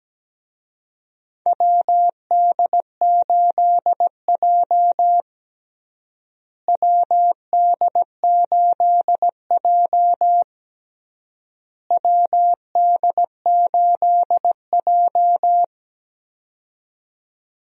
Morse code, heard as a keyed tone, about 17 wpm, 700 Hz. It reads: WD8J WD8J WD8J
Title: USA Call Signs - 4 Characters - Repeated 3x - 17wpm